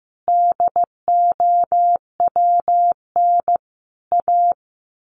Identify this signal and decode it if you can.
Morse code, heard as a keyed tone, about 15 wpm, 700 Hz. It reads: DOWN A
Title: A Sound Of Thunder 15wpm